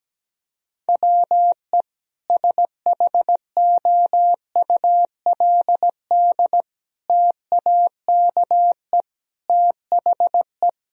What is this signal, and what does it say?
Morse code, heard as a keyed tone, about 17 wpm, 700 Hz. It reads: WE SHOULD TAKE THE
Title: Sentences from Top 1000 Words - Review Part 2 - 17wpm